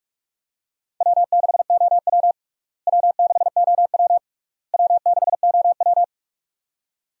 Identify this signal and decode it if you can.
Morse code, heard as a keyed tone, about 46 wpm, 700 Hz. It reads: W6OW W6OW W6OW
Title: USA Call Signs - 4 Characters - Repeated 3x - 45wpm